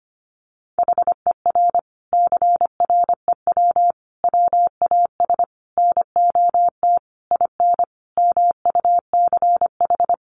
Morse code, heard as a keyed tone, about 25 wpm, 700 Hz. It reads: HER CREW WAS NOT IN MUCH
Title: Sentences from Top 1500 Words 25wpm